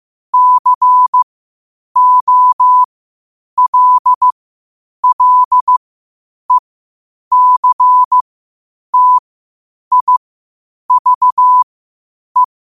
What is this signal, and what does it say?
Morse code, heard as a keyed tone, about 15 wpm, spaced as for 8 wpm, 1000 Hz. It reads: COLLECTIVE
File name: qotdfiles / 8wpm